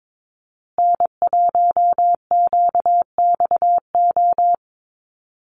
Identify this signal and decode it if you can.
Morse code, heard as a keyed tone, about 22 wpm, 700 Hz. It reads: N1QXO
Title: USA Call Signs Repeated 2x 22wpm